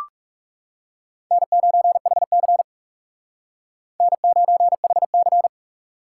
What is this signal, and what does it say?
Morse code, heard as a keyed tone, about 36 wpm, 700 Hz. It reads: N9HC N9HC
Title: Speed Racing - USA Call Signs 30wpm